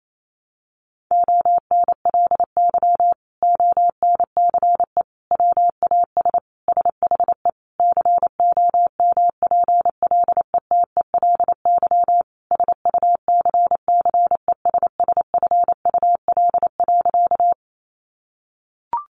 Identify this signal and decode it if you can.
Morse code, heard as a keyed tone, about 28 wpm, 700 Hz. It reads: ONLY ONCE WAS SHE COMPLETELY SUCCESSFUL.
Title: Sentences from Top 1100 Words 28wpm